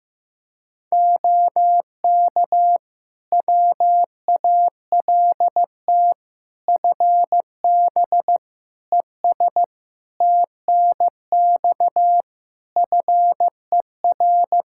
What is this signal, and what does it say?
Morse code, heard as a keyed tone, about 15 wpm, 700 Hz. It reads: OK WALT FB ES TNX FER
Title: Practice QSO - Exchange 2 - Info - 15wpm